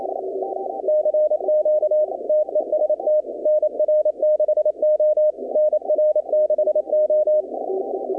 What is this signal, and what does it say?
Morse code, heard as a keyed tone, about 27 wpm, 610 Hz. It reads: CQ TEST NR6O NR6O
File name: contests / 2019-cw-wpx-kt5j-k5pi- -2330 0-20